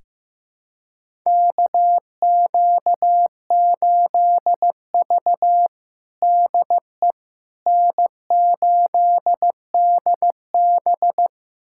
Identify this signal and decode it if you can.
Morse code, heard as a keyed tone, about 15 wpm, 700 Hz. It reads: KQ8V DE N8DB
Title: USA Callsign DE Callsign - 4 Characters - 15wpm